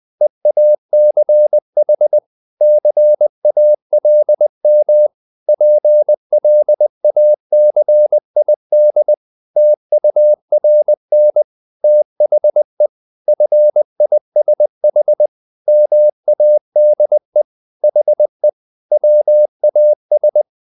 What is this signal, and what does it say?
Morse code, heard as a keyed tone, about 20 wpm, 600 Hz. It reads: EACH CALM PLACID TURN THE FISH MADE HE WAS
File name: the old man and the sea part 2 20wpm